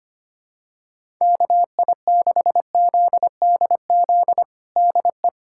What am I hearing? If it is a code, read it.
Morse code, heard as a keyed tone, about 25 wpm, 700 Hz. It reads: KI6ZDZ DE
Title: USA Callsign DE Callsign 25wpm